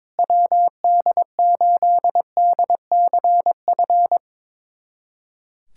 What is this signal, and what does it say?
Morse code, heard as a keyed tone, about 22 wpm, 700 Hz. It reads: WD8DCF
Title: USA Callsign DE Callsign - 6 Characters - 22wpm